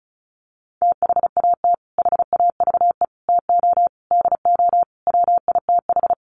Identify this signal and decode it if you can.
Morse code, heard as a keyed tone, about 34 wpm, 700 Hz. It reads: THAT HAVE TO DO WITH
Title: Sets of 5 Words Encore2 35wpm